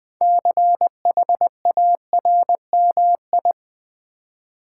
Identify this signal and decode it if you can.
Morse code, heard as a keyed tone, about 20 wpm, 700 Hz. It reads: CHARMI
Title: Build A Word - 3 Letters At A Time - 8-Letter Words - 20wpm